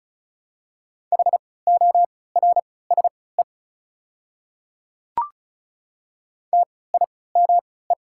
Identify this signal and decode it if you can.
Morse code, heard as a keyed tone, about 34 wpm, spaced as for 20 wpm, 700 Hz. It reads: HORSE TIME
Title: 3x Character Spacing - Top 300 Words - 35wpm